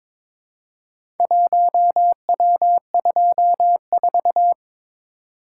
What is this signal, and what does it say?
Morse code, heard as a keyed tone, about 23 wpm, 690 Hz. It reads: 1W24
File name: practice / 22wpm / Four Letter-Number Combinations - 22wpm